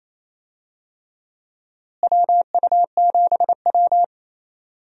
Morse code, heard as a keyed tone, about 28 wpm, 700 Hz. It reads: WU7W
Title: USA Call Signs - 4 Characters - Repeated 3x - 28wpm